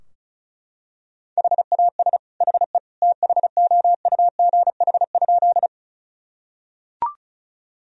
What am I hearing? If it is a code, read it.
Morse code, heard as a keyed tone, about 34 wpm, 700 Hz. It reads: HAS HE THOUGH?
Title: Sentences from Top 1000 Words - Review Part 2 - 35wpm